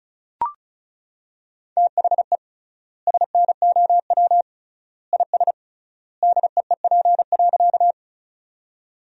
Morse code, heard as a keyed tone, about 34 wpm, 700 Hz. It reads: THE SNOW IS DEEP.
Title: 3x Word-Spacing - Sentences from Top 500 Words - 35wpm